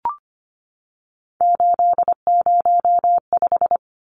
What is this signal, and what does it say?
Morse code, heard as a keyed tone, about 24 wpm, 700 Hz. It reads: 805
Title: Numbers - 3-Digit 25wpm